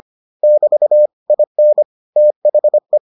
Morse code, heard as a keyed tone, about 25 wpm, 600 Hz. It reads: <BT> IN THE